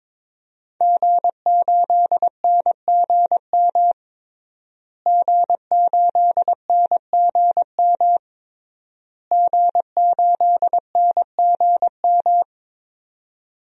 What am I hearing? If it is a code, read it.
Morse code, heard as a keyed tone, about 22 wpm, 700 Hz. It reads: G8NGM G8NGM G8NGM
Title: DXCC-100 Call Signs - 5 Characters - Repeated 3x - 22wpm